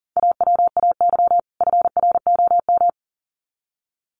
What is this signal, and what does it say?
Morse code, heard as a keyed tone, about 40 wpm, 700 Hz. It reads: AWAY FROM
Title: Sets of 2 Words 40wpm